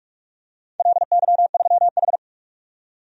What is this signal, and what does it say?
Morse code, heard as a keyed tone, about 47 wpm, 700 Hz. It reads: LY3H
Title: DXCC-100 Call Signs - 4 Characters - 45wpm